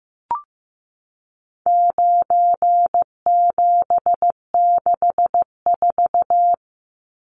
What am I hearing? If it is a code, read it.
Morse code, heard as a keyed tone, about 15 wpm, 700 Hz. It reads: 9764
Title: Numbers - 4-Digit 15wpm